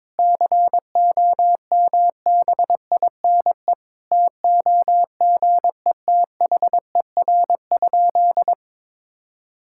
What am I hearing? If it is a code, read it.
Morse code, heard as a keyed tone, about 22 wpm, 700 Hz. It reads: COMBINE TOGETHER?
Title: Sentences from Top 1600 Words 22wpm